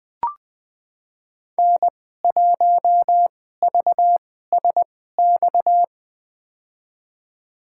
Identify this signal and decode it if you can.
Morse code, heard as a keyed tone, about 20 wpm, spaced as for 14 wpm, 700 Hz. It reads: N1VSX
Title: USA Call Signs - 5 Characters - 2x Character Spacing - 20wpm